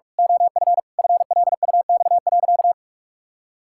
Morse code, heard as a keyed tone, about 43 wpm, 700 Hz. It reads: OF FLUX.
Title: Pangram Challenge v2 - 45wpm